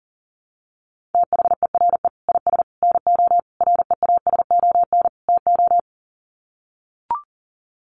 Morse code, heard as a keyed tone, about 40 wpm, 700 Hz. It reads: THERE IS NO REASON TO
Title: Sets of 5 Words Encore 40wpm